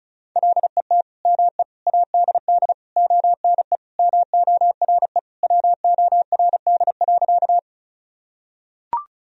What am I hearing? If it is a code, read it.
Morse code, heard as a keyed tone, about 36 wpm, 700 Hz. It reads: LET ME ADD ONE MORE WORD.
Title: Sentences from Top 1000 Words - Review Part 1 - 35wpm